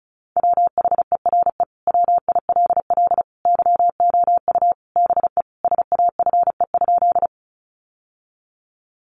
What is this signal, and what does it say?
Morse code, heard as a keyed tone, about 34 wpm, 700 Hz. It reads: WHERE WILL YOU BE SAFE?
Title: Sentences from Top 900 Words 35wpm